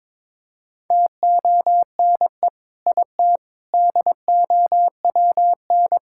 Morse code, heard as a keyed tone, about 22 wpm, 700 Hz. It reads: TONE IT DOWN
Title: Sentences from Top 1600 Words 22wpm